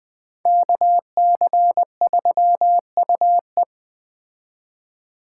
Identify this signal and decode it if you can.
Morse code, heard as a keyed tone, about 22 wpm, 700 Hz. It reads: KC3UE